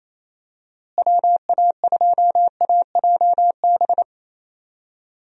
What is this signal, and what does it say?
Morse code, heard as a keyed tone, about 28 wpm, 700 Hz. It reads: WA2AJB
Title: USA Call Signs - 6 Characters - Repeated 3x - 28wpm